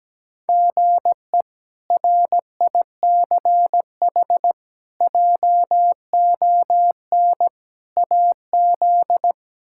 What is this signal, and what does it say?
Morse code, heard as a keyed tone, about 17 wpm, 700 Hz. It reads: GE RICH JON AZ
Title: K1USN SST QSO Practice - 17wpm